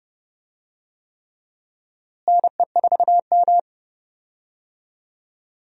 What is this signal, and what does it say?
Morse code, heard as a keyed tone, about 30 wpm, 700 Hz. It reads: NE4M
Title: USA Call Signs - 4 Characters - Repeated 3x - 30wpm